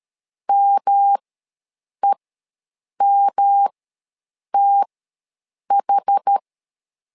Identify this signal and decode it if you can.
Morse code, heard as a keyed tone, about 13 wpm, spaced as for 7 wpm, 800 Hz. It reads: MEMTH
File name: LEZIONI / 3 MWRO / 3 M ws 12wpm 1745